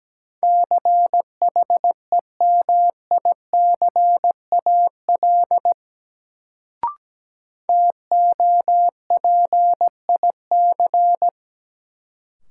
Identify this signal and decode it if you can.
Morse code, heard as a keyed tone, about 17 wpm, 700 Hz. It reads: CHEMICAL TOPIC
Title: Sentences from Top 1200 Words - New Word Drill Alt - 17wpm